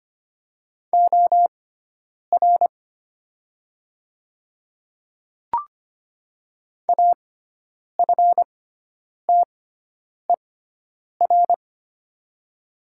Morse code, heard as a keyed tone, about 25 wpm, spaced as for 9 wpm, 700 Hz. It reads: OR AFTER
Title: 6x Character Spacing - Top 200 Words - 25wpm